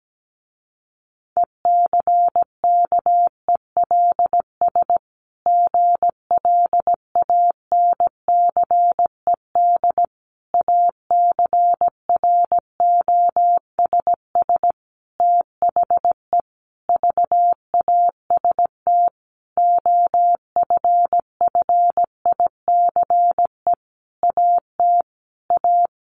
Morse code, heard as a keyed tone, about 17 wpm, 700 Hz. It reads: ECKELS GLANCED ACROSS THE VAST OFFICE AT A
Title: A Sound Of Thunder 17wpm